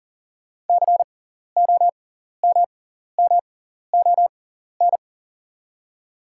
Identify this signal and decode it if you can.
Morse code, heard as a keyed tone, about 40 wpm, spaced as for 14 wpm, 690 Hz. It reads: COMMON
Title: 6x Character Spacing - Top 500 Words - 40wpm